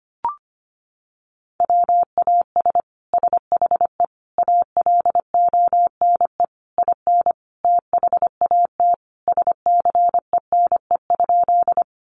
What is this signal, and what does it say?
Morse code, heard as a keyed tone, about 25 wpm, 700 Hz. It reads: WAS SHE ALONE IN THAT SCENE?
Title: Sentences from Top 1100 Words 25wpm